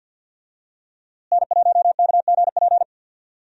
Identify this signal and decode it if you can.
Morse code, heard as a keyed tone, about 51 wpm, 700 Hz. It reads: N1KGP
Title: USA Call Signs - 5 Characters - Repeated 3x - 50wpm